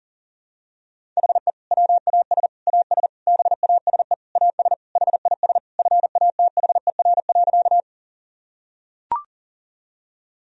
Morse code, heard as a keyed tone, about 40 wpm, 700 Hz. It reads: HE WAS AS BASE AS HIS FATHER.
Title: Sentences from Top 1000 Words - Review Part 1 - 40wpm